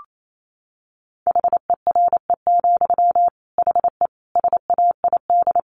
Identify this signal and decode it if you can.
Morse code, heard as a keyed tone, about 28 wpm, 700 Hz. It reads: HERE, HE SAID
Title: Sentences from Top 1500 Words 28wpm